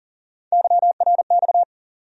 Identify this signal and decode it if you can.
Morse code, heard as a keyed tone, about 38 wpm, 700 Hz. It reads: YRX
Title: USA Call Sign Postfixes - 3 Letters 40wpm